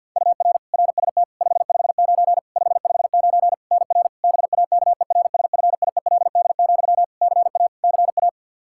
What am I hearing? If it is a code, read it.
Morse code, heard as a keyed tone, about 48 wpm, 700 Hz. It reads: UR RST 559 559 NR BAKERSFIELD, CA CA
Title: Practice QSO - Exchange 1 - Beginning - 50wpm